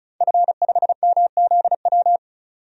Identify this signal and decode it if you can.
Morse code, heard as a keyed tone, about 36 wpm, 700 Hz. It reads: F5MZW
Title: DXCC-100 Call Signs - 5 Characters - 35wpm